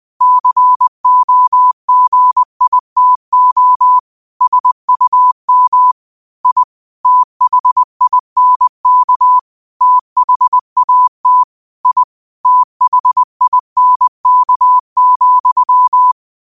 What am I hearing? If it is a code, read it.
Morse code, heard as a keyed tone, about 20 wpm, 1000 Hz. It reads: COGITO SUM I THINK THAT I THINK,
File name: qotdfiles / 20wpm